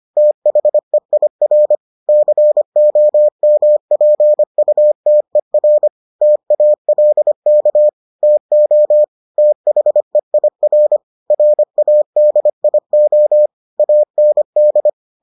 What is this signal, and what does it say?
Morse code, heard as a keyed tone, about 25 wpm, 600 Hz. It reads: THEIR COMPUTER TALK TO THEIR RADIO AND